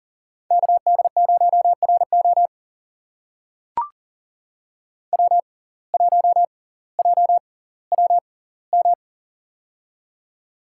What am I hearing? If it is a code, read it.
Morse code, heard as a keyed tone, about 40 wpm, spaced as for 20 wpm, 700 Hz. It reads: KD0RO W1JWM
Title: USA Call Signs - 5 Characters - 6x Character Spacing - 40wpm